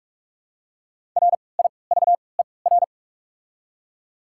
Morse code, heard as a keyed tone, about 45 wpm, spaced as for 25 wpm, 690 Hz. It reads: RIVER
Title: 3x Character Spacing - Top 300 Words - 45wpm